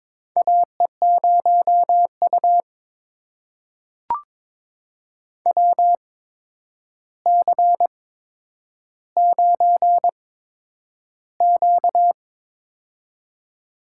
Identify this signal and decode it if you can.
Morse code, heard as a keyed tone, about 22 wpm, spaced as for 9 wpm, 700 Hz. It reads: AE0U WC9Q
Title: USA Call Signs - 4 Characters - 8x Character Spacing - 22wpm